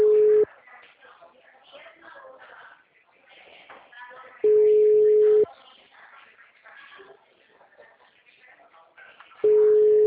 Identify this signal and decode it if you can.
Ringback tone (Europe, 425 Hz), with speech.
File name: out- -2011- - - .4519